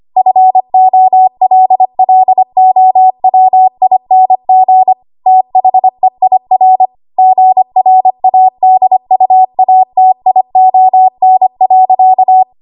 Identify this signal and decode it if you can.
Morse code, heard as a keyed tone, about 25 wpm, 750 Hz. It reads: FOLLOWING THEIR GRADUATION.
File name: Archive / 25 WPM / 25WPM